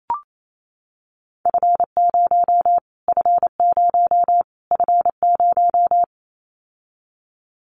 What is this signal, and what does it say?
Morse code, heard as a keyed tone, about 28 wpm, 700 Hz. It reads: F0 F0 F0
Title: Callsign Prefixes for Top 25 DXCC - Repeated 3x - 28wpm